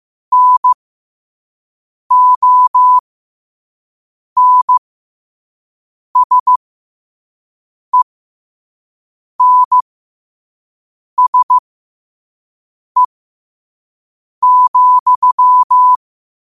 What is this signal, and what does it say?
Morse code, heard as a keyed tone, about 15 wpm, spaced as for 5 wpm, 1000 Hz. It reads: NONSENSE,
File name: 2024-10-07-5wpm